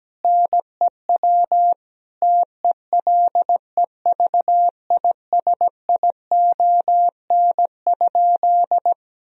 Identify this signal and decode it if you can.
Morse code, heard as a keyed tone, about 17 wpm, 700 Hz. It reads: NEW TELEVISION?
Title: Sentences from Top 1000 Words - Review Part 4 - 17wpm